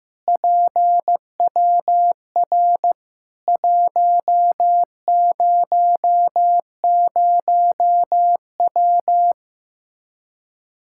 Morse code, heard as a keyed tone, about 15 wpm, 700 Hz. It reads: PWR 100W
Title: Practice QSO - Exchange 2 - Info - 15wpm